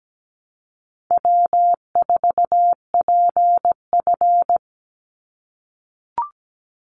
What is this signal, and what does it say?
Morse code, heard as a keyed tone, about 17 wpm, 700 Hz. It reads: W4PF
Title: Easy Call Signs - 17wpm